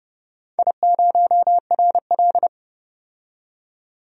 Morse code, heard as a keyed tone, about 30 wpm, 700 Hz. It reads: I0RL
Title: DXCC-100 Call Signs - 4 Characters - 30wpm